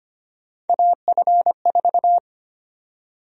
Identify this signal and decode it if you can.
Morse code, heard as a keyed tone, about 25 wpm, 700 Hz. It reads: AF4
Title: USA Call Sign Prefixes - 25wpm